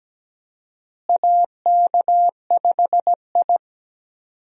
Morse code, heard as a keyed tone, about 17 wpm, 700 Hz. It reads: AK5I
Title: USA Call Signs - 4 Characters - Repeated 3x - 17wpm